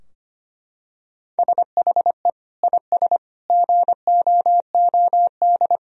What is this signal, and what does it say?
Morse code, heard as a keyed tone, about 25 wpm, 700 Hz. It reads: SHE IS GOOD
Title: Sentences from Top 1000 Words - Review Part 3 - 25wpm